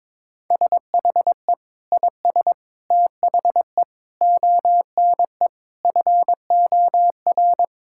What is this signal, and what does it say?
Morse code, heard as a keyed tone, about 22 wpm, 700 Hz. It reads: SHE IS THE ONE FOR
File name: practice / 22wpm / Sentences with 2-Word Phrases - 22wpm